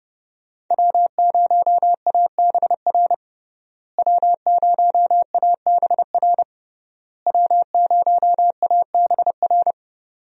Morse code, heard as a keyed tone, about 32 wpm, 700 Hz. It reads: W0ABR W0ABR W0ABR
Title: USA Call Signs - 5 Characters - Repeated 3x - 30wpm